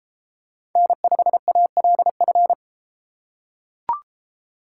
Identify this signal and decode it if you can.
Morse code, heard as a keyed tone, about 32 wpm, 700 Hz. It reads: N5ALF